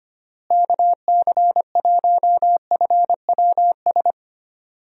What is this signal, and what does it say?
Morse code, heard as a keyed tone, about 25 wpm, 700 Hz. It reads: KC1FWS
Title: USA Call Signs - 6 Characters - Repeated 2x - 25wpm